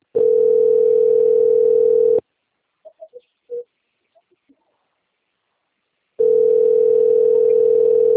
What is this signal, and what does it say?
Ringback tone (North America), 440 + 480 Hz.